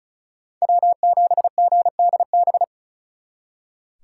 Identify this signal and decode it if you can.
Morse code, heard as a keyed tone, about 47 wpm, 700 Hz. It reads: W7GDB